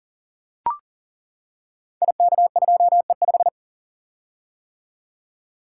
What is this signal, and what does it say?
Morse code, heard as a keyed tone, about 38 wpm, 700 Hz. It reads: IK2E5
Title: Code Groups - 5 Letter:Numbers - 40wpm